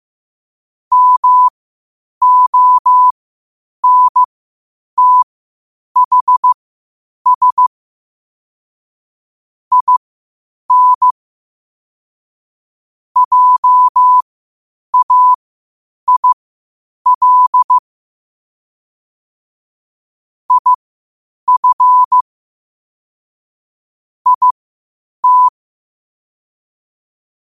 Morse code, heard as a keyed tone, about 15 wpm, spaced as for 8 wpm, 1000 Hz. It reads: MONTHS IN JAIL IF IT
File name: qotdfiles / 8wpm